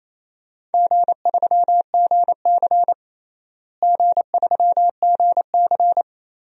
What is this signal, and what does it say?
Morse code, heard as a keyed tone, about 28 wpm, 700 Hz. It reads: G3GC G3GC
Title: DXCC-100 Call Signs - 4 Characters - Repeated 3x - 28wpm